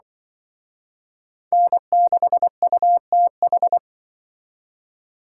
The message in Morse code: N6UTH

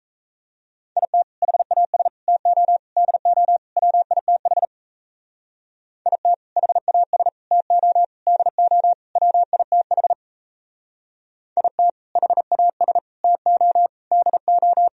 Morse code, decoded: IT HAS TO DO WITH IT HAS TO DO WITH IT HAS TO DO

